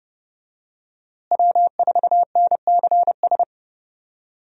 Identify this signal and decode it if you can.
Morse code, heard as a keyed tone, about 32 wpm, 700 Hz. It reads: W4NCS
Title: USA Call Signs - 5 Characters - Repeated 3x - 30wpm